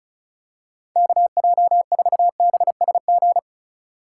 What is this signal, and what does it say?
Morse code, heard as a keyed tone, about 36 wpm, 700 Hz. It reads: KJ4BSG
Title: USA Call Signs - 6 Characters - 2x Character Spacing - 35wpm